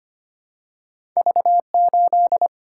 Morse code, heard as a keyed tone, about 25 wpm, 700 Hz. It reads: V8
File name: Two Letter-Number Combinations - 25wpm